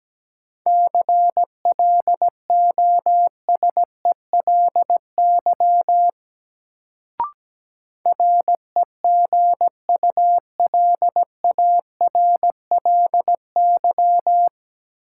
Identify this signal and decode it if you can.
Morse code, heard as a keyed tone, about 17 wpm, 700 Hz. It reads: CLOSELY REGULARLY
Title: Sentences from Top 1400 Words - New Word Drill - 17wpm